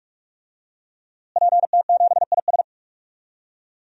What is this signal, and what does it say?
Morse code, heard as a keyed tone, about 42 wpm, 700 Hz. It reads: PT7IS
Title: DXCC-100 Call Signs - 5 Characters - Repeated 2x - 45wpm